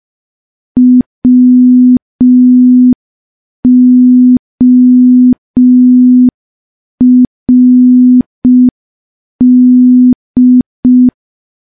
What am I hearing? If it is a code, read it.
Morse code, heard as a keyed tone, about 5 wpm, 250 Hz. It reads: WORD
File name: morse / morse-training-05WPM